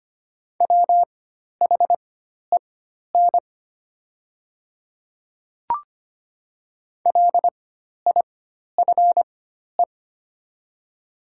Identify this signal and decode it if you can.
Morse code, heard as a keyed tone, about 25 wpm, spaced as for 12 wpm, 690 Hz. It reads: WHEN LIFE
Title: 4x Character Spacing - Top 500 Words - 25wpm